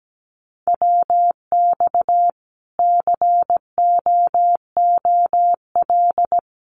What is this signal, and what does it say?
Morse code, heard as a keyed tone, about 17 wpm, 700 Hz. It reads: WX COOL
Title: Common QSO Phrases 17wpm